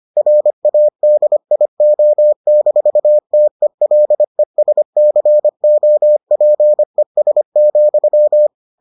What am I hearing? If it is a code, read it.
Morse code, heard as a keyed tone, about 25 wpm, 600 Hz. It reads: RADIO-TELESCOPES,